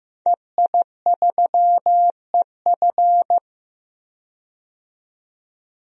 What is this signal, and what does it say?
Morse code, heard as a keyed tone, about 15 wpm, 700 Hz. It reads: EI3EF
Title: DXCC-100 Call Signs - 5 Characters - 15wpm